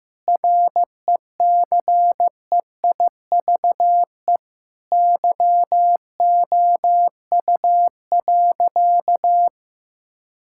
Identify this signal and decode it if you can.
Morse code, heard as a keyed tone, about 15 wpm, 700 Hz. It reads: RECEIVE YOU.
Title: Sentences from Top 1600 Words 15wpm